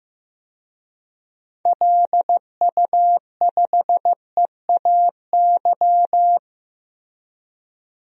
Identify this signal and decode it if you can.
Morse code, heard as a keyed tone, about 15 wpm, 700 Hz. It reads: LU5EAY
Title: DXCC-100 Call Signs - 6 Characters - Repeated 3x - 15wpm